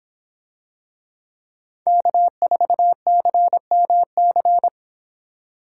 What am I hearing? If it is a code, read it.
Morse code, heard as a keyed tone, about 26 wpm, 700 Hz. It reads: K4CMC